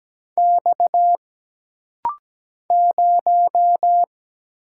Morse code, heard as a keyed tone, about 17 wpm, 700 Hz. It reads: X 0
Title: Single Letter-Number - Rapid-Fire 17wpm